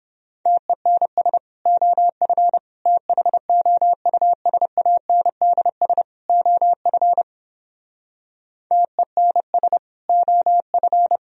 Morse code, heard as a keyed tone, about 29 wpm, 700 Hz. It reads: TENS OF THOUSANDS OF TENS OF